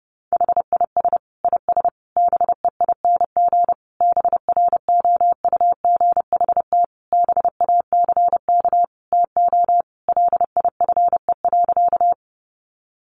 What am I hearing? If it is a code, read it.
Morse code, heard as a keyed tone, about 30 wpm, 700 Hz. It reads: HIS IS BEING BROUGHT BACK TO LIFE.
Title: Sentences from Top 600 Words 30wpm